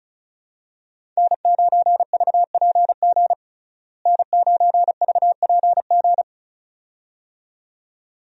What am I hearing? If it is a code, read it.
Morse code, heard as a keyed tone, about 36 wpm, 700 Hz. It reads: N9VPG N9VPG
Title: USA Call Signs - 5 Characters - Repeated 2x - 35wpm